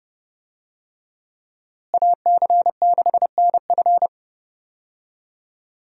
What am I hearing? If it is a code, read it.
Morse code, heard as a keyed tone, about 30 wpm, 700 Hz. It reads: AC6NF
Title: USA Call Signs - 5 Characters - Repeated 2x - 30wpm